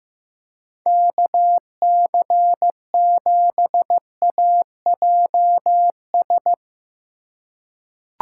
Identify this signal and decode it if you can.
Morse code, heard as a keyed tone, about 15 wpm, 700 Hz. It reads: KC7AJS
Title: USA Call Signs - 6 Characters - 2x Character Spacing - 15wpm